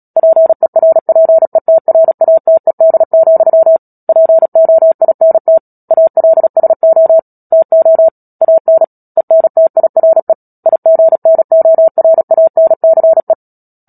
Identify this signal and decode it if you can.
Morse code, heard as a keyed tone, about 37 wpm, 650 Hz. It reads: PERPETRATED, POINT ALSO TO AN ENTIRE IGNORANCE
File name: Book Two / 10-Chapter 10 - Epilogue - 36 WPM